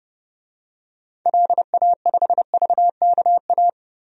Morse code, heard as a keyed tone, about 32 wpm, 700 Hz. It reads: LA5VKA
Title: DXCC-100 Call Signs - 6 Characters - 30wpm